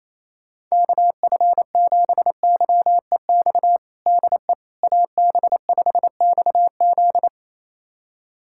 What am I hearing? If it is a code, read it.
Morse code, heard as a keyed tone, about 28 wpm, 700 Hz. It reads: KF7YEX DE AB5XZ
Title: USA Callsign DE Callsign 28wpm